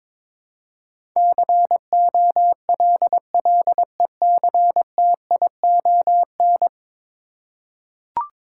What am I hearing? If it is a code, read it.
Morse code, heard as a keyed tone, about 21 wpm, 700 Hz. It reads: COLLECTION
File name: practice / 22wpm / 10 Letter Words - First 3 Letters Given - 22wpm